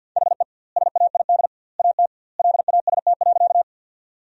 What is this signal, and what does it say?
Morse code, heard as a keyed tone, about 50 wpm, 700 Hz. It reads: HE SAID AT LAST.
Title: 2x Word-Spacing - Sentences from Top 300 Words - 50wpm